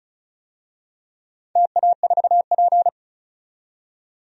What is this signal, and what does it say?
Morse code, heard as a keyed tone, about 36 wpm, 700 Hz. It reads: TA4P
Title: DXCC-100 Call Signs - 4 Characters - 35wpm